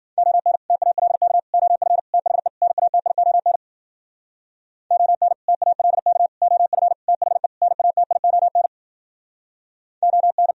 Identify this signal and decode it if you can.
Morse code, heard as a keyed tone, about 51 wpm, 700 Hz. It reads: ON TALK OF THE NATION ON TALK OF THE NATION ON